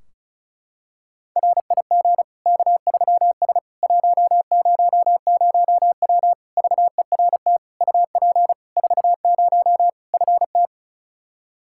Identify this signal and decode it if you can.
Morse code, heard as a keyed tone, about 36 wpm, 700 Hz. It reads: RIG K3S 100W VERT UP 40 FT
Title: K1USN SST QSO Practice 35wpm